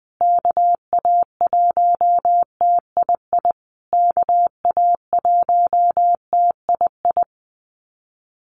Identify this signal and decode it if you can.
Morse code, heard as a keyed tone, about 19 wpm, 700 Hz. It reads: KA1TII KA1TII
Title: USA Call Signs Repeated 3x 20wpm